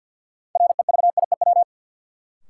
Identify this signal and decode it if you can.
Morse code, heard as a keyed tone, about 53 wpm, 700 Hz. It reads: REVIEW
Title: Sentences from Top 900 Words - New Word Drill - 50wpm